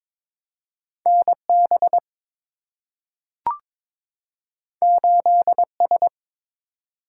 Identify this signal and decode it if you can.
Morse code, heard as a keyed tone, about 22 wpm, 700 Hz. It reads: NB 8S
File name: practice / 22wpm / Two Letter-Number Combinations - 22wpm